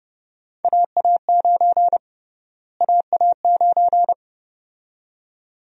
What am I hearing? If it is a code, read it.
Morse code, heard as a keyed tone, about 30 wpm, 700 Hz. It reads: AA9 AA9
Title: USA Call Sign Prefixes - Repeated 2x - 30wpm